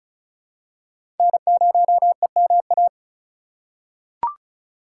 Morse code, heard as a keyed tone, about 35 wpm, 700 Hz. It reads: N0EMA
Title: USA Call Signs - 5 Characters - 2x Character Spacing - 35wpm